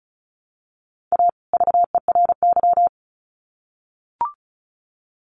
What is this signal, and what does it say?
Morse code, heard as a keyed tone, about 34 wpm, 700 Hz. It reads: A VERY
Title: Sets of 2 Words 35wpm